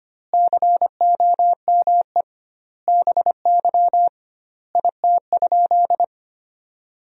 Morse code, heard as a keyed tone, about 25 wpm, 700 Hz. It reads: COME BY IT?
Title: 2x Word-Spacing - Sentences from Top 100 Words - 25wpm